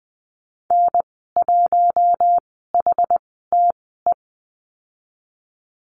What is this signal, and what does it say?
Morse code, heard as a keyed tone, about 20 wpm, spaced as for 15 wpm, 700 Hz. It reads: N1HTE
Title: 082 Lesson 14g - Number 2 - Call Signs 20wpm